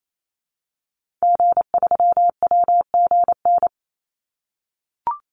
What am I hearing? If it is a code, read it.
Morse code, heard as a keyed tone, about 28 wpm, 700 Hz. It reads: G3WGN
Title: Call Signs - 28wpm